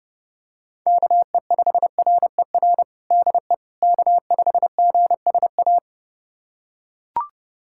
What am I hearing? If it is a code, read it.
Morse code, heard as a keyed tone, about 30 wpm, 700 Hz. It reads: KE5RER DE K5GSA
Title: USA Callsign DE Callsign 30wpm